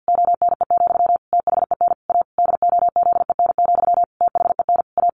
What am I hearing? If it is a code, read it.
Morse code, heard as a keyed tone, about 50 wpm, 700 Hz. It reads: ONE, THEN A DOZEN, THEN A
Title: A Sound Of Thunder 50wpm